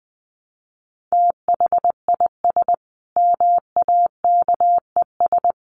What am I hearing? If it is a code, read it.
Morse code, heard as a keyed tone, about 20 wpm, 700 Hz. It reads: THIS MAKES
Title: A Sound Of Thunder 20wpm